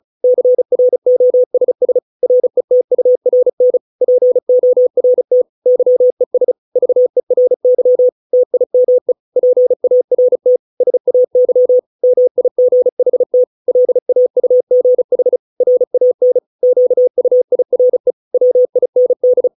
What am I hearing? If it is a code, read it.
Morse code, heard as a keyed tone, about 34 wpm, 500 Hz. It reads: CROSS RETURN PORT YES VERY TIME PART SAY MIGHT LAUGH RAN QUIRE WIND